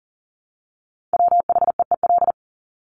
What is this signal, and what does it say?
Morse code, heard as a keyed tone, about 40 wpm, 700 Hz. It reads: WHEEL
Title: Top 500 Words Repeated 40wpm